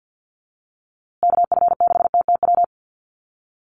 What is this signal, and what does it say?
Morse code, heard as a keyed tone, about 50 wpm, 700 Hz. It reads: KF6TTW
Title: USA Call Signs Repeated 3x 50wpm